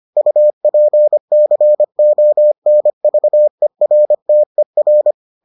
Morse code, heard as a keyed tone, about 25 wpm, 600 Hz. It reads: UPCONVERTER